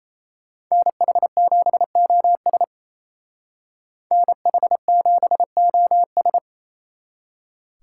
Morse code, heard as a keyed tone, about 34 wpm, 700 Hz. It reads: NH7OS NH7OS
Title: Speed Racing - USA 2x2 Call Signs 28wpm